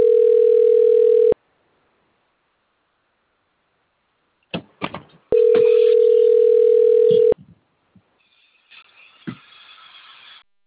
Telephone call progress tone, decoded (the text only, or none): ringback tone